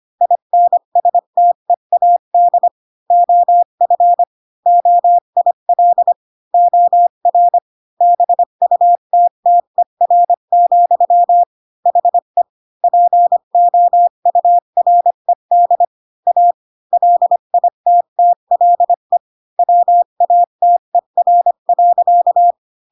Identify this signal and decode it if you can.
Morse code, heard as a keyed tone, about 25 wpm, 700 Hz. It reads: INSTEAD OF OIL OR BUTTER, HE POURED A LITTLE WATER.
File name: Pinocchio / Pinocchio ch5 20 25 ik1hge